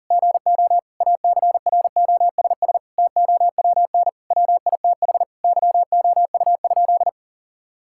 Morse code, heard as a keyed tone, about 40 wpm, 700 Hz. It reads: GO ACROSS TOWN WITH YOU?
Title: Sentences from Top 1000 Words - Review Part 5 - 40wpm